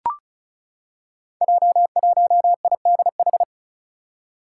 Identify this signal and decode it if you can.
Morse code, heard as a keyed tone, about 36 wpm, 700 Hz. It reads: J1IDH
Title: Five Challenging Letter-Number Combinations 35wpm